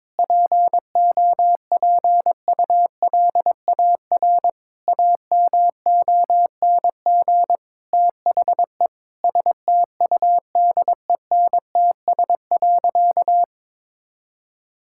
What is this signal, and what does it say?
Morse code, heard as a keyed tone, about 22 wpm, 700 Hz. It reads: POPULAR AMONG THE STUDENTS.
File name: practice / 22wpm / Sentences with 2-Word Phrases - 22wpm